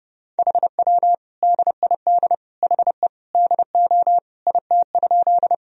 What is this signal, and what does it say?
Morse code, heard as a keyed tone, about 30 wpm, 700 Hz. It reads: HW DID HE DO IT?
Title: Sentences from Top 100 Words and Abbreviations 30wpm